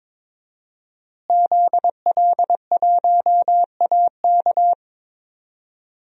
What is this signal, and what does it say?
Morse code, heard as a keyed tone, about 21 wpm, 700 Hz. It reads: ZL1AK
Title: DXCC-100 Call Signs - 5 Characters - Repeated 3x - 22wpm